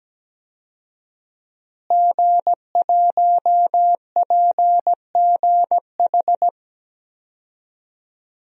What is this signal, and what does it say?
Morse code, heard as a keyed tone, about 17 wpm, 700 Hz. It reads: G1PGH
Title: DXCC-100 Call Signs - 5 Characters - Repeated 2x - 17wpm